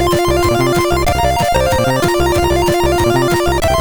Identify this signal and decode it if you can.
There is no code.